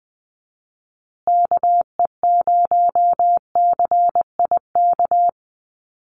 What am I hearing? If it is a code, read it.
Morse code, heard as a keyed tone, about 20 wpm, 700 Hz. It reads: KE0CIK
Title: USA Call Signs Repeated 3x 20wpm